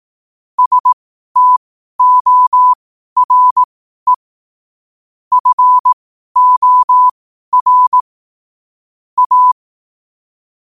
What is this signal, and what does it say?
Morse code, heard as a keyed tone, about 18 wpm, spaced as for 12 wpm, 1000 Hz. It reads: STORE FOR A
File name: qotdfiles / 12wpm